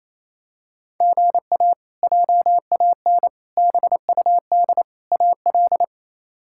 Morse code, heard as a keyed tone, about 27 wpm, 700 Hz. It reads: GA JAN BUD AL
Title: K1USN SST QSO Practice - 28wpm